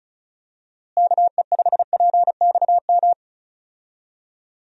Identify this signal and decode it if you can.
Morse code, heard as a keyed tone, about 36 wpm, 700 Hz. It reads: KE5PXM
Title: USA Call Signs - 6 Characters - Repeated 3x - 35wpm